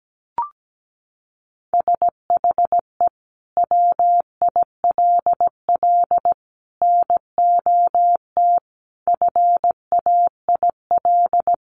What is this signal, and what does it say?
Morse code, heard as a keyed tone, about 17 wpm, 700 Hz. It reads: SHE WILL NOT FAIL
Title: Sentences from Top 1100 Words 17wpm